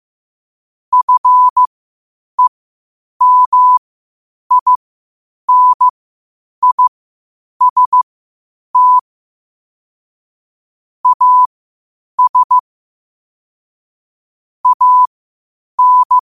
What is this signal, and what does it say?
Morse code, heard as a keyed tone, about 15 wpm, spaced as for 8 wpm, 1000 Hz. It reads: FEMINIST AS AN
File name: qotdfiles / 8wpm